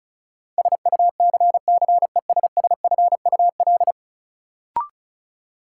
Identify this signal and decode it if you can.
Morse code, heard as a keyed tone, about 36 wpm, 700 Hz. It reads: SUCCESSFUL
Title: Sentences from Top 900 Words - New Word Drill - 35wpm